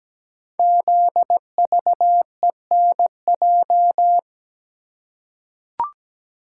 Morse code, heard as a keyed tone, about 17 wpm, 700 Hz. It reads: ZVENJ
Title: Code Groups - 5 Letters - 17wpm